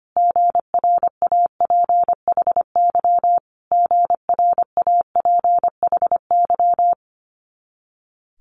Morse code, heard as a keyed tone, about 25 wpm, 700 Hz. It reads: GRAPHY GRAPHY
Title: Multi-letter Phonemes Repeated 3x 25wpm